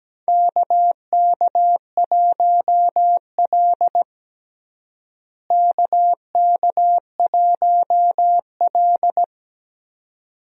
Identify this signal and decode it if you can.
Morse code, heard as a keyed tone, about 17 wpm, 700 Hz. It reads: KK1L KK1L
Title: USA Call Signs - 4 Characters - Repeated 2x - 17wpm